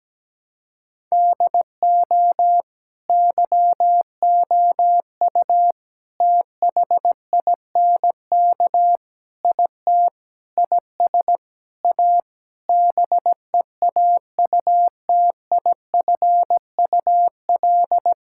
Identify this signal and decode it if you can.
Morse code, heard as a keyed tone, about 17 wpm, 700 Hz. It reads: DO YOU THINK IT IS A BEAUTIFUL
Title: Sentences from Top 1000 Words - Review Part 6 - 17wpm